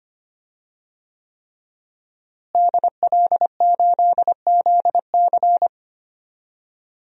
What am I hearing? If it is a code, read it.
Morse code, heard as a keyed tone, about 25 wpm, 700 Hz. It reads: DL8ZC
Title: DXCC-100 Call Signs - 5 Characters - Repeated 3x - 25wpm